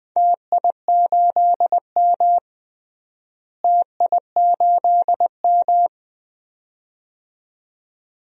Morse code, heard as a keyed tone, about 20 wpm, 700 Hz. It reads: TI8M TI8M
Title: DXCC-100 Call Signs - 4 Characters - Repeated 2x - 20wpm